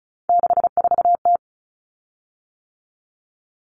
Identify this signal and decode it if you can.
Morse code, heard as a keyed tone, about 34 wpm, 700 Hz. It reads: 64T
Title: Common Cut Numbers - 3-Digit 35wpm